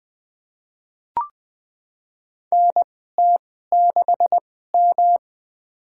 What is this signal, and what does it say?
Morse code, heard as a keyed tone, about 20 wpm, spaced as for 14 wpm, 700 Hz. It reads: NT6M
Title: USA Call Signs - 2x Character Spacing - 20wpm